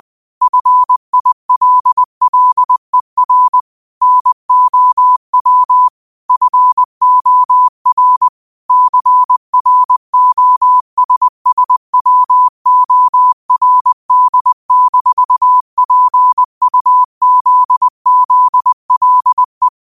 Morse code, heard as a keyed tone, about 20 wpm, 1000 Hz. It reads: FILLER NOW FOR CROSSWORD-PUZZLE
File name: qotdfiles / 20wpm